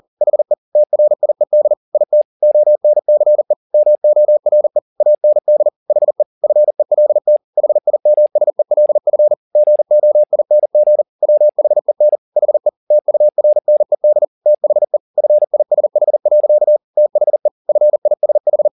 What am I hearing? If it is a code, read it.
Morse code, heard as a keyed tone, about 40 wpm, 600 Hz. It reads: HE TRIED IT ONCE MORE AND HE FELT HIMSELF GOING WHEN HE TURNED THE FISH. THE FISH
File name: the old man and the sea 2 40wpm